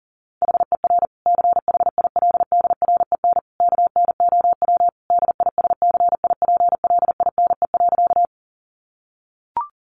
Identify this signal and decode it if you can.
Morse code, heard as a keyed tone, about 40 wpm, 700 Hz. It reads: HER CHILDREN KNOW DISCIPLINE.
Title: Sentences from Top 900 Words 40wpm